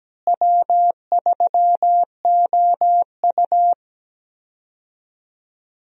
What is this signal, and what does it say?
Morse code, heard as a keyed tone, about 17 wpm, 700 Hz. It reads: W3OU
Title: USA Call Signs - 4 Characters - Repeated 2x - 17wpm